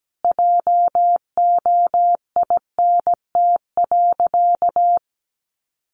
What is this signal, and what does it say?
Morse code, heard as a keyed tone, about 17 wpm, 700 Hz. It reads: JOINT.
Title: Sentences from Top 1300 Words 17wpm